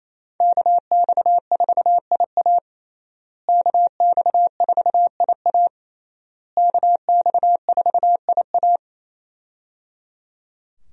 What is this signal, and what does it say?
Morse code, heard as a keyed tone, about 27 wpm, 700 Hz. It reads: KX4IA KX4IA KX4IA
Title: USA Call Signs - 5 Characters - Repeated 3x - 28wpm